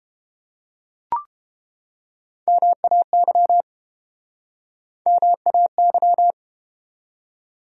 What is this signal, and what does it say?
Morse code, heard as a keyed tone, about 30 wpm, 700 Hz. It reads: MAY MAY